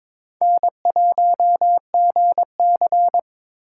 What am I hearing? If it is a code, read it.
Morse code, heard as a keyed tone, about 27 wpm, 700 Hz. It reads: N1GC
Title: Speed Racing - USA 1x2 Call Signs 22wpm